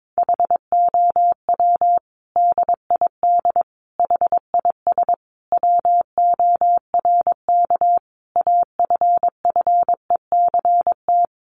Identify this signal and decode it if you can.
Morse code, heard as a keyed tone, about 22 wpm, 700 Hz. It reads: HOW DID HIS WORK AFFECT
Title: Sentences from Top 900 Words 22wpm